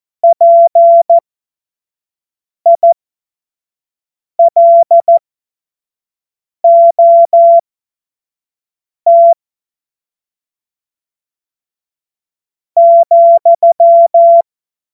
Morse code, heard as a keyed tone, about 14 wpm, spaced as for 4 wpm, 680 Hz. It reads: PILOT,
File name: Morsekurs / Witze / Witz 20. Juni 2023 25 F70